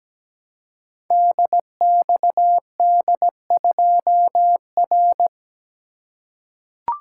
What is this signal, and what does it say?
Morse code, heard as a keyed tone, about 17 wpm, 700 Hz. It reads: DXD2R
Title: Code Groups - 5 Letter:Numbers - 17wpm